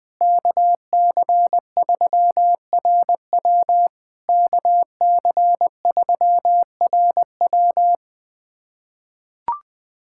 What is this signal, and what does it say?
Morse code, heard as a keyed tone, about 20 wpm, 700 Hz. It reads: KC3RW KC3RW
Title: POTA QSO Practice - 20wpm